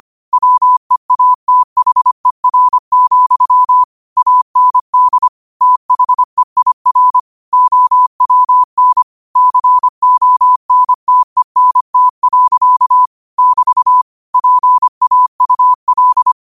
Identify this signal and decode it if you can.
Morse code, heard as a keyed tone, about 25 wpm, 1000 Hz. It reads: WEATHER, AND THEIR OWN CONTENT. <BT> PAUL